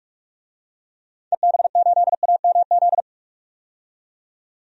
Morse code, heard as a keyed tone, about 46 wpm, 700 Hz. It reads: EB8AMZ
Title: DXCC-100 Call Signs - 6 Characters - 45wpm